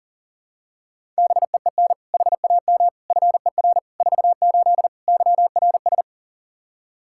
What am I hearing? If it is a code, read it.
Morse code, heard as a keyed tone, about 40 wpm, 700 Hz. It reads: BEEN HAM FER 48 YRS
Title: Practice QSO - Exchange 3 - More Info - 40wpm